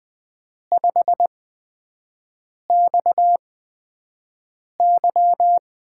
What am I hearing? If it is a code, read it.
Morse code, heard as a keyed tone, about 20 wpm, spaced as for 5 wpm, 700 Hz. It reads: 5XY
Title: USA Call Signs - 4 Characters - 8x Character Spacing - 20wpm